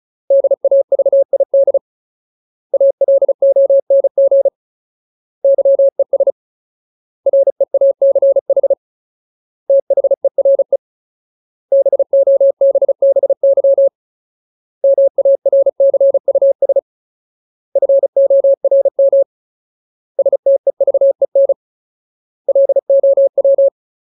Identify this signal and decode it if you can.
Morse code, heard as a keyed tone, about 34 wpm, 550 Hz. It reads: DAVID ALONG YES REACH THERE BOBBY MARCUS FORM STEVEN LOW
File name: Ultimate Word List / 35WPM Ultimate